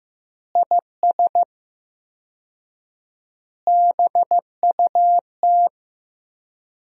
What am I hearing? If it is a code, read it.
Morse code, heard as a keyed tone, about 15 wpm, 700 Hz. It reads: IS BUT
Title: 4x Word-Spacing - Sentences from Top 100 Words - 15wpm